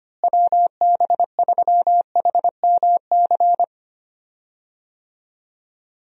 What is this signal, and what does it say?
Morse code, heard as a keyed tone, about 25 wpm, 700 Hz. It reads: WB3HMC
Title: USA Call Signs - 6 Characters - Repeated 2x - 25wpm